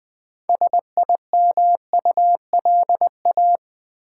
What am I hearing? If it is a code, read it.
Morse code, heard as a keyed tone, about 20 wpm, 700 Hz. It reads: SIMULA
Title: Build A Word - 3 Letters At A Time - 8-Letter Words - 20wpm